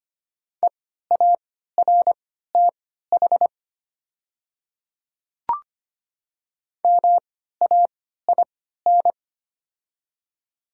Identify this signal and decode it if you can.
Morse code, heard as a keyed tone, about 25 wpm, spaced as for 14 wpm, 690 Hz. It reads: EARTH MAIN
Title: 3x Character Spacing - Top 300 Words - 25wpm